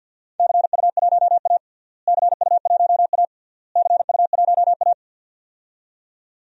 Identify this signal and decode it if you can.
Morse code, heard as a keyed tone, about 53 wpm, 700 Hz. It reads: CU1A CU1A CU1A
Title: DXCC-100 Call Signs - 4 Characters - Repeated 3x - 50wpm